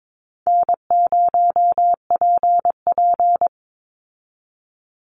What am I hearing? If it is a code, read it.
Morse code, heard as a keyed tone, about 22 wpm, 700 Hz. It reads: N0PP
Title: Easy Call Signs Repeated 2x - 22wpm